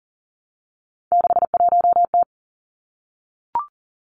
Morse code, heard as a keyed tone, about 41 wpm, 700 Hz. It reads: 61T
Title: Three Challenging Letter-Number Combinations 40wpm